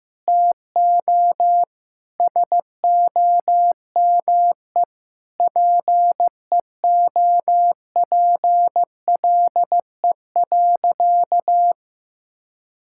Morse code, heard as a keyed tone, about 15 wpm, 700 Hz. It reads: TO SOME PEOPLE.
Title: Sentences from Top 1000 Words - Review Part 5 - 15wpm